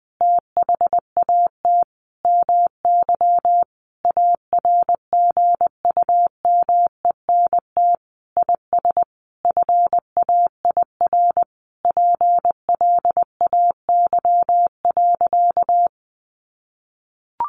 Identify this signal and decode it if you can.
Morse code, heard as a keyed tone, about 20 wpm, 700 Hz. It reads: THAT MY ARGUMENT IS FAIR PLAY.
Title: Sentences from Top 1400 Words 20wpm